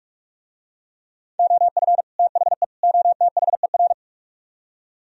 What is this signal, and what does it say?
Morse code, heard as a keyed tone, about 47 wpm, 700 Hz. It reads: OF THE OTHER